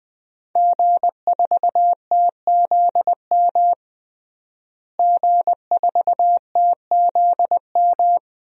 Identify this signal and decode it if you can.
Morse code, heard as a keyed tone, about 20 wpm, 700 Hz. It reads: G4TZM G4TZM
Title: DXCC-100 Call Signs - 5 Characters - Repeated 2x - 20wpm